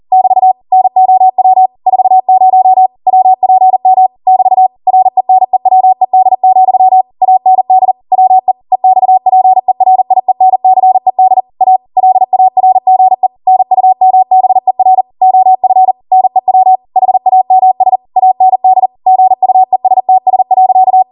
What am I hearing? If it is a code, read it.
Morse code, heard as a keyed tone, about 40 wpm, 750 Hz. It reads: <BT> NOW 40 WPM <BT> RENEWED, AND WE EXPERIENCED A LARGE NUMBER OF NEW HAMS AND GUESTS.